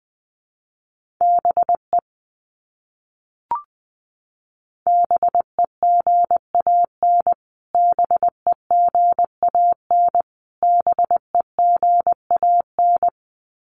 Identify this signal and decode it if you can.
Morse code, heard as a keyed tone, about 20 wpm, 700 Hz. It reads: BE BEGAN BEGAN BEGAN
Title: Top 500 Words Repeated 20wpm